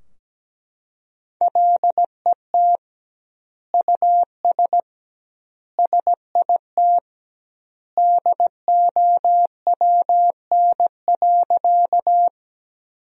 Morse code, heard as a keyed tone, about 17 wpm, 700 Hz. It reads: LET US SIT DOWN.
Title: 2x Word-Spacing - Sentences from Top 500 Words - 17wpm